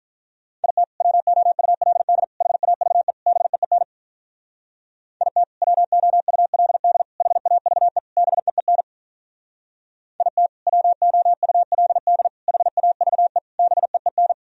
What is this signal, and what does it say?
Morse code, heard as a keyed tone, about 46 wpm, 700 Hz. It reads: IT WOULD HAVE BEEN IT WOULD HAVE BEEN IT WOULD HAVE BEEN